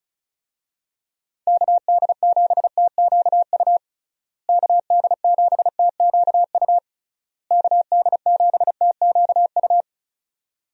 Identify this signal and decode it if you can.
Morse code, heard as a keyed tone, about 34 wpm, 700 Hz. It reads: KD7TQU KD7TQU KD7TQU
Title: USA Call Signs - 6 Characters - Repeated 3x - 35wpm